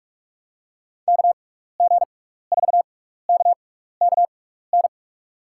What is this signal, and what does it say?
Morse code, heard as a keyed tone, about 44 wpm, spaced as for 15 wpm, 700 Hz. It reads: KG4KKN
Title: USA Call Signs - 6 Characters - 6x Character Spacing - 45wpm